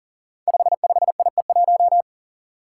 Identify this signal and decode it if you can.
Morse code, heard as a keyed tone, about 41 wpm, 700 Hz. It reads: 55IE1
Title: Five Challenging Letter-Number Combinations 40wpm